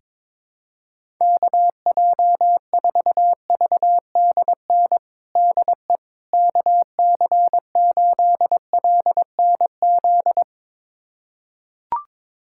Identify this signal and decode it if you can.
Morse code, heard as a keyed tone, about 21 wpm, 700 Hz. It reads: KJ4VDN DE KC8LNZ
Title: USA Callsign DE Callsign - 6 Characters - 22wpm